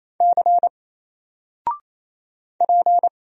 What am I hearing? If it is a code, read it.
Morse code, heard as a keyed tone, about 28 wpm, 700 Hz. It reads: C P
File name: Single-Letter-Number-28wpm